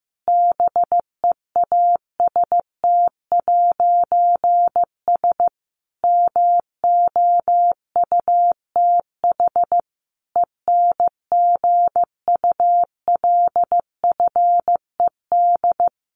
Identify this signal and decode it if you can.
Morse code, heard as a keyed tone, about 15 wpm, 700 Hz. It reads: BEAST'S MOUTH ENGULFED
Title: A Sound Of Thunder 15wpm